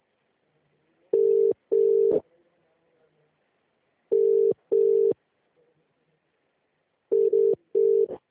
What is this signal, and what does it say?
Ringback tone (UK), 400 + 450 Hz.